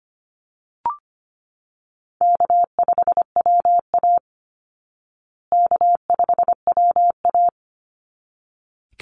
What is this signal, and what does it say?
Morse code, heard as a keyed tone, about 25 wpm, 700 Hz. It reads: K5WA K5WA
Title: Easy Call Signs Repeated 2x - 25wpm